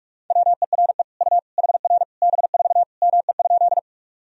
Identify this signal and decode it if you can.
Morse code, heard as a keyed tone, about 46 wpm, 700 Hz. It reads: WERE U HR B4 ME?
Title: Sentences from Top 200 Words and Abbreviations 45wpm